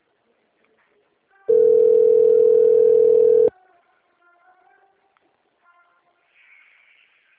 Ringback tone (North America), 440 + 480 Hz.